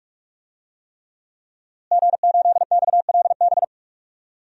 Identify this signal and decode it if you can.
Morse code, heard as a keyed tone, about 46 wpm, 700 Hz. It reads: G8XLB
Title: DXCC-100 Call Signs - 5 Characters - Repeated 3x - 45wpm